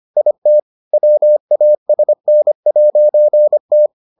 Morse code, heard as a keyed tone, about 25 wpm, 600 Hz. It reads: IT WASN'T